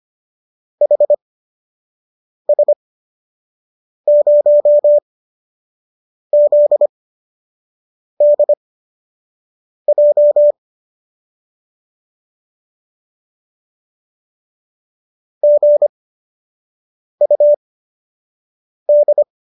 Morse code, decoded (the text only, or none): HS0ZDJ GUD